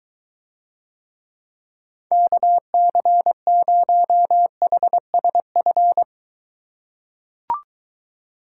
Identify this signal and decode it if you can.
Morse code, heard as a keyed tone, about 23 wpm, 700 Hz. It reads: KC0HSF